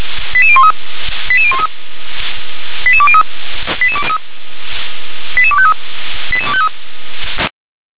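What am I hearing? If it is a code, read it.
Five-tone selective call (ZVEI), heard as quick runs of five tones, tones of about 70 ms. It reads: 88013 88013 88203 88253 88253
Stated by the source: ZVEI five-tone selective call received on SDR.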